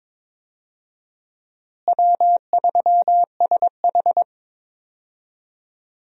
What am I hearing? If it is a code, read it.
Morse code, heard as a keyed tone, about 22 wpm, 700 Hz. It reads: W3SH
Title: USA Call Signs - 4 Characters - Repeated 3x - 22wpm